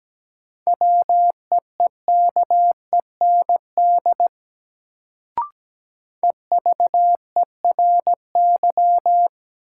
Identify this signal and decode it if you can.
Morse code, heard as a keyed tone, about 17 wpm, 700 Hz. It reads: WEEKEND EVERY
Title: Sets of 5 Words - New Word Drill - 17wpm